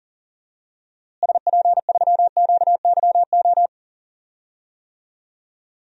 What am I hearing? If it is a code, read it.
Morse code, heard as a keyed tone, about 40 wpm, 700 Hz. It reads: SP3QYO
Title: DXCC-100 Call Signs - 6 Characters - Repeated 3x - 40wpm